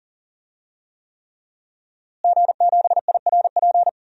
Morse code, heard as a keyed tone, about 40 wpm, 700 Hz. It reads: G7IRP
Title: DXCC-100 Call Signs - 5 Characters - Repeated 3x - 40wpm